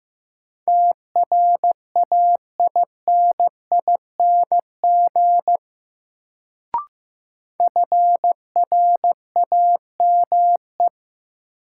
Morse code, heard as a keyed tone, about 15 wpm, 700 Hz. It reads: TRAINING FRAME
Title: Sentences from Top 1400 Words - New Word Drill Alt - 15wpm